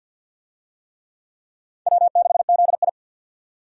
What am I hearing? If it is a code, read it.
Morse code, heard as a keyed tone, about 52 wpm, 700 Hz. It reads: W6ZI
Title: USA Call Signs - 4 Characters - 50wpm